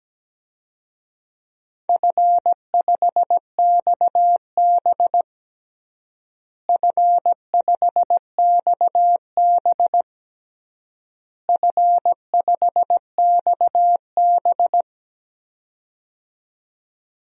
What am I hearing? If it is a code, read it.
Morse code, heard as a keyed tone, about 17 wpm, 700 Hz. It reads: F5XB F5XB F5XB
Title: DXCC-100 Call Signs - 4 Characters - Repeated 3x - 17wpm